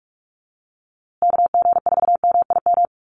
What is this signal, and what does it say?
Morse code, heard as a keyed tone, about 44 wpm, 700 Hz. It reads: KG4MIM
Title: USA Call Signs Repeated 3x 45wpm